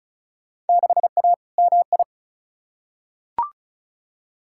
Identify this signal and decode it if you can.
Morse code, heard as a keyed tone, about 36 wpm, 700 Hz. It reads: 6A MI
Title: Field Day Exchanges - 35wpm